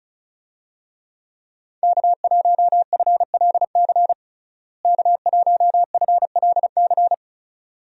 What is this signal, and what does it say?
Morse code, heard as a keyed tone, about 36 wpm, 700 Hz. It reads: K1FLC K1FLC
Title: USA Call Signs - 5 Characters - Repeated 2x - 35wpm